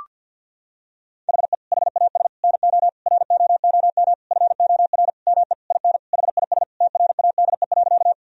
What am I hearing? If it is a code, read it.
Morse code, heard as a keyed tone, about 50 wpm, 700 Hz. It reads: HE HAS NO ROOM FOR ME IN HIS TRADE.
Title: Sentences from Top 1000 Words - Review Part 4 - 50wpm